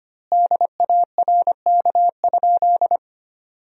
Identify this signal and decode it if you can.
Morse code, heard as a keyed tone, about 25 wpm, 700 Hz. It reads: DARK?
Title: 3x Word-Spacing - Sentences from Top 500 Words - 25wpm